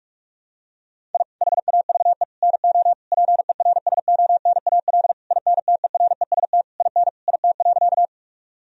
Morse code, heard as a keyed tone, about 46 wpm, 700 Hz. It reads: I HAVE NO PERSONAL INTEREST IN IT.
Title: Sentences from Top 1000 Words - Review Part 4 - 45wpm